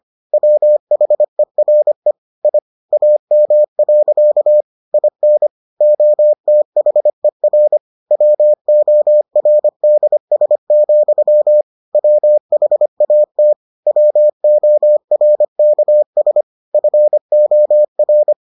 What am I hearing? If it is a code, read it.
Morse code, heard as a keyed tone, about 25 wpm, 600 Hz. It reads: WHERE I AM. IN OTHER WORDS, WHAT WORKS FOR